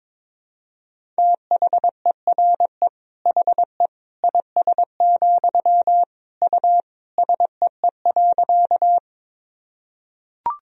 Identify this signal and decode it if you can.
Morse code, heard as a keyed tone, about 21 wpm, 700 Hz. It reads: THERE HE IS, U SEE.
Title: Sentences from Top 100 Words and Abbreviations 22wpm